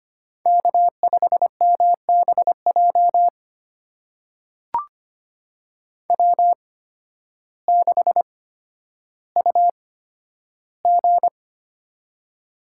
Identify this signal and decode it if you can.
Morse code, heard as a keyed tone, about 25 wpm, spaced as for 12 wpm, 700 Hz. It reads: K5MBJ W6UG
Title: USA Call Signs - 8x Character Spacing - 25wpm